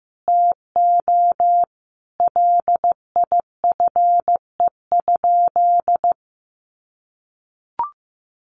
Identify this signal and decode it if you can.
Morse code, heard as a keyed tone, about 15 wpm, 700 Hz. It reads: TO LIFE?
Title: Sentences from Top 1500 Words 15wpm